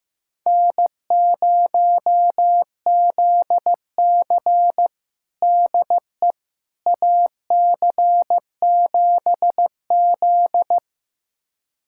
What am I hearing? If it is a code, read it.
Morse code, heard as a keyed tone, about 15 wpm, 700 Hz. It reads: N0ZC DE AC7Z
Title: USA Callsign DE Callsign - 4 Characters - 15wpm